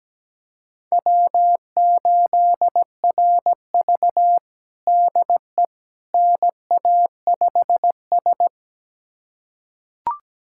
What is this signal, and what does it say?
Morse code, heard as a keyed tone, about 17 wpm, 700 Hz. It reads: W8RV DE NA5S
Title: USA Callsign DE Callsign - 4 Characters - 17wpm